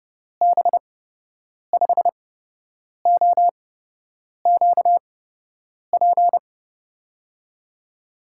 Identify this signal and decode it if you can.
Morse code, heard as a keyed tone, about 30 wpm, spaced as for 8 wpm, 700 Hz. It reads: B5OQP
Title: USA Call Signs - 6 Characters - 8x Character Spacing - 30wpm